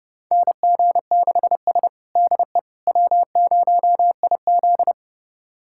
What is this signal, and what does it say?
Morse code, heard as a keyed tone, about 30 wpm, 700 Hz. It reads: NG6S DE W0IZ
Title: USA Callsign DE Callsign - 4 Characters - 30wpm